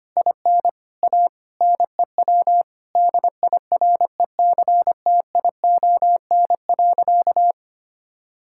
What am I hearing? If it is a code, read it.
Morse code, heard as a keyed tone, about 25 wpm, 700 Hz. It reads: IN A NEW DIRECTION.
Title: Sentences from Top 1000 Words - Review Part 5 - 25wpm